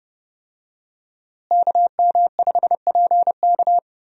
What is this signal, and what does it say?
Morse code, heard as a keyed tone, about 30 wpm, 700 Hz. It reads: KM5PK
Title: USA Call Signs - 5 Characters - Repeated 3x - 30wpm